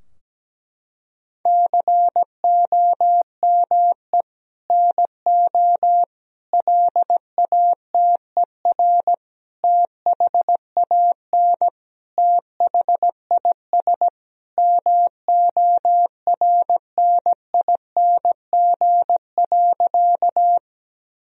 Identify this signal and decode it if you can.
Morse code, heard as a keyed tone, about 17 wpm, 700 Hz. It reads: COME NO LATER THAN THIS MORNING.
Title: Sentences from Top 1000 Words - Review Part 4 - 17wpm